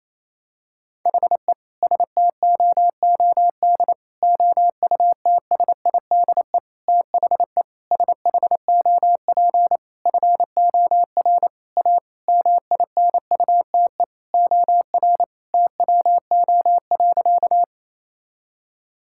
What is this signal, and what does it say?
Morse code, heard as a keyed tone, about 28 wpm, 700 Hz. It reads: HE STOOD OUTSIDE THE SHOP FOR A MINUTE OR TWO.
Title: Sentences from Top 1600 Words 28wpm